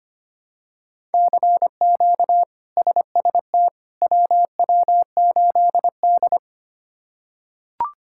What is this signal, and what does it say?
Morse code, heard as a keyed tone, about 25 wpm, 700 Hz. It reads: CQ SST WW8D
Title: K1USN SST QSO Practice - 25wpm